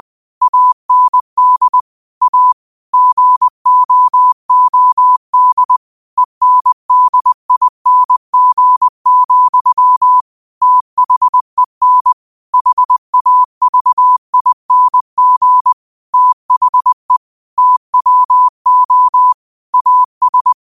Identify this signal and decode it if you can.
Morse code, heard as a keyed tone, about 20 wpm, 1000 Hz. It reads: AND A GOOD ENDING, THEN HAVING THE TWO AS